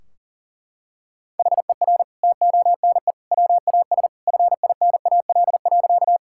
Morse code, heard as a keyed tone, about 40 wpm, 700 Hz. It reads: HER TONE WAS FINAL.
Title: Sentences from Top 1600 Words 40wpm